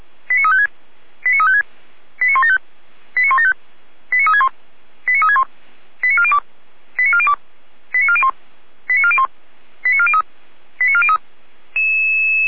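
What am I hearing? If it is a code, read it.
Five-tone selective call (ZVEI), heard as quick runs of five tones, tones of about 70 ms. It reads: 89356 89356 89175 89175 89361 89361 89402 89402 89401 89401 89403 89403
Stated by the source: ZVEI five-tone call 89172 received by SDR.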